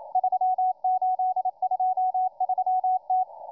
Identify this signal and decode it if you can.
Morse code, heard as a keyed tone, about 27 wpm, 740 Hz, with radio noise. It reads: 3823T